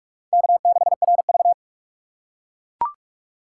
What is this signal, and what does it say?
Morse code, heard as a keyed tone, about 44 wpm, 700 Hz. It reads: K6RV
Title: USA Call Signs - 4 Characters - 2x Character Spacing - 45wpm